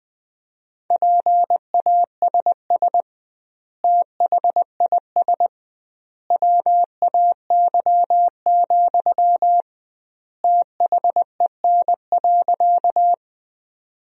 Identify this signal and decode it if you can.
Morse code, heard as a keyed tone, about 20 wpm, 700 Hz. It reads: PASS THIS WAY, THEN.
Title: 2x Word-Spacing - Sentences from Top 400 Words - 20wpm